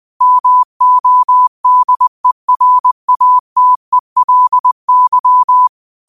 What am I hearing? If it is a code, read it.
Morse code, heard as a keyed tone, about 20 wpm, 1000 Hz. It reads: MODERATELY
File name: qotdfiles / 20wpm